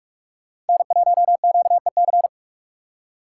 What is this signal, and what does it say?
Morse code, heard as a keyed tone, about 57 wpm, 700 Hz. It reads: N1QEC